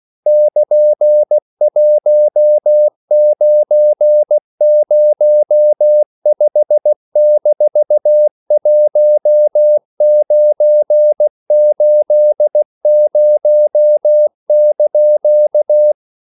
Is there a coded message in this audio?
Yes: Morse code from 0.3 to 15.9 s, heard as a keyed tone, about 16 wpm, 600 Hz. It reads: <KN>1905-1980)